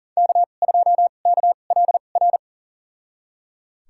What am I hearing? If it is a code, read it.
Morse code, heard as a keyed tone, about 39 wpm, spaced as for 29 wpm, 700 Hz. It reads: K2KLR
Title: USA Call Signs - 2x Character Spacing - 40wpm